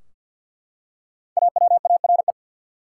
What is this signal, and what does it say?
Morse code, heard as a keyed tone, about 52 wpm, 700 Hz. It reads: AWARE
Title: Five Letter Words - 50wpm